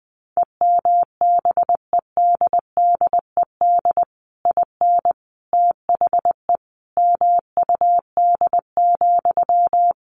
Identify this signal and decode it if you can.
Morse code, heard as a keyed tone, about 20 wpm, 700 Hz. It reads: EMBEDDED IN THE MUD,
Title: A Sound Of Thunder 20wpm